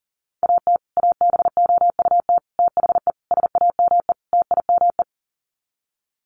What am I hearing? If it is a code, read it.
Morse code, heard as a keyed tone, about 40 wpm, 700 Hz. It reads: AT ABOUT THE SAME TIME
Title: Sets of 5 Words Encore 40wpm